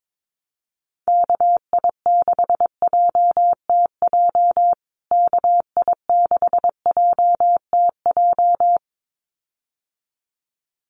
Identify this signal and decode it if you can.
Morse code, heard as a keyed tone, about 22 wpm, 700 Hz. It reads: KI6JTJ KI6JTJ
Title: USA Call Signs Repeated 2x 22wpm